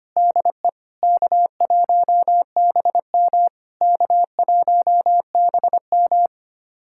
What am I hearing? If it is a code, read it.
Morse code, heard as a keyed tone, about 25 wpm, 660 Hz. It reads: DE K1BM K1BM
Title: SOTA QSO Practice 25wpm